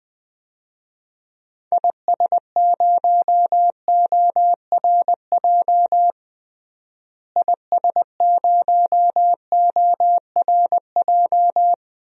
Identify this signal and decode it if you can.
Morse code, heard as a keyed tone, about 20 wpm, 700 Hz. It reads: IS0ORJ IS0ORJ
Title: DXCC-100 Call Signs - 6 Characters - Repeated 2x - 20wpm